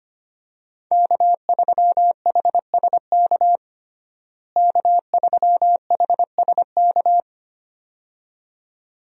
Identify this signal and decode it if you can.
Morse code, heard as a keyed tone, about 25 wpm, 700 Hz. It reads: K3HSK K3HSK
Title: USA Call Signs - 5 Characters - Repeated 2x - 25wpm